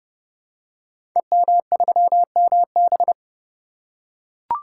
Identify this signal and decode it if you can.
Morse code, heard as a keyed tone, about 30 wpm, 700 Hz. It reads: EM3MB
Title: Code Groups - 5 Letter:Numbers - 30wpm